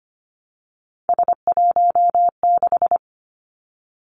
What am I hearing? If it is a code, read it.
Morse code, heard as a keyed tone, about 25 wpm, 700 Hz. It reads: S16
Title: Three Challenging Letter-Number Combinations 25wpm